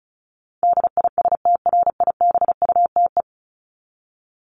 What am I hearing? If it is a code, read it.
Morse code, heard as a keyed tone, about 34 wpm, 700 Hz. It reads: DISTRIBUTE
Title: Top 1501-1750 Words 35wpm